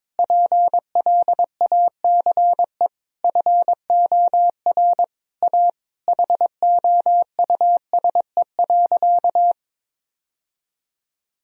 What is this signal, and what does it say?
Morse code, heard as a keyed tone, about 22 wpm, 700 Hz. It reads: PLACE FOR A HOUSE.
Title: Sentences from Top 1000 Words - Review Part 3 - 22wpm